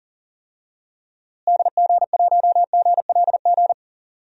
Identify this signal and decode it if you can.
Morse code, heard as a keyed tone, about 40 wpm, 700 Hz. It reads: DG1GLG
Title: DXCC-100 Call Signs - 6 Characters - Repeated 2x - 40wpm